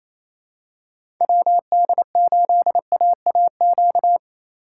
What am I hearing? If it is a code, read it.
Morse code, heard as a keyed tone, about 27 wpm, 700 Hz. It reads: WD8AAQ
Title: USA Call Signs - 2x Character Spacing - 28wpm